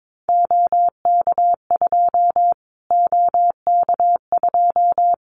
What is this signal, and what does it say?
Morse code, heard as a keyed tone, about 22 wpm, 700 Hz. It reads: OK2 OK2
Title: Callsign Prefixes for Top 25 DXCC - Repeated 3x - 22wpm